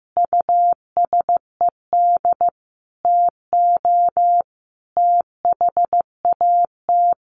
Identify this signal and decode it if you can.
Morse code, heard as a keyed tone, about 14 wpm, 700 Hz. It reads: USED TO THAT
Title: Sentences from Top 700 Words 15wpm